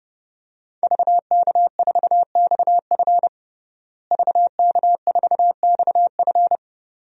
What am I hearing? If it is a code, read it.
Morse code, heard as a keyed tone, about 30 wpm, 700 Hz. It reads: VK4XF VK4XF
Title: DXCC-100 Call Signs - 5 Characters - Repeated 2x - 30wpm